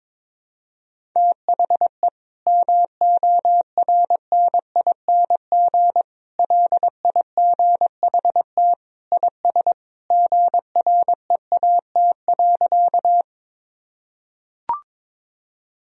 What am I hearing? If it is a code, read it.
Morse code, heard as a keyed tone, about 22 wpm, 700 Hz. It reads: THE MORNING LIGHT IS GREAT.
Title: Sentences from Top 1000 Words - Review Part 1 - 22wpm